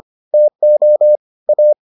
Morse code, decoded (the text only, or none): TO A